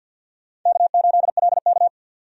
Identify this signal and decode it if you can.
Morse code, heard as a keyed tone, about 47 wpm, 700 Hz. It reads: K8LK
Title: USA Call Signs - 4 Characters - 50wpm